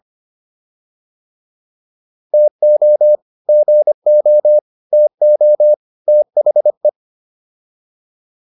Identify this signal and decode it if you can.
Morse code, heard as a keyed tone, about 25 wpm, 600 Hz. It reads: TO GO TO THE